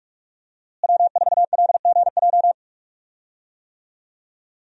Morse code, heard as a keyed tone, about 43 wpm, 700 Hz. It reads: W4LGJ
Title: USA Call Signs - 5 Characters - Repeated 2x - 45wpm